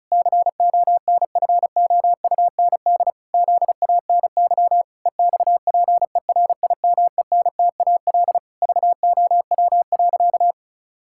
Morse code, heard as a keyed tone, about 36 wpm, 700 Hz. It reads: CONFOUND ZANY EXPERIMENTAL VOW.
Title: Pangram Challenge v2 - 35wpm